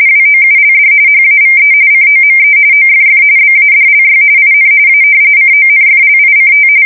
RTTY, about 45.45 baud. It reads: WOULD HAVE COME OFF BADLY IN A CONFLICT
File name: fsk170 / usnavy3